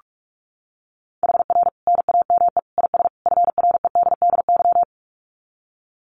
Morse code, heard as a keyed tone, about 44 wpm, 700 Hz. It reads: HR NAME IS FREDDY
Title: Common QSO Phrases 45wpm